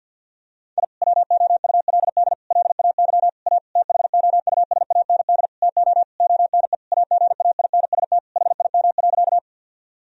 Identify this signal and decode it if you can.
Morse code, heard as a keyed tone, about 50 wpm, 700 Hz. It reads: I WOULD LAY A THOUSAND TO ONE AGAINST HIM.
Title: Sentences from Top 1000 Words - Review Part 3 - 50wpm